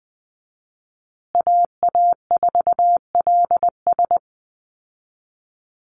Morse code, heard as a keyed tone, about 20 wpm, 700 Hz. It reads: AA4LS
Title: Call Signs Repeated 2x - 20wpm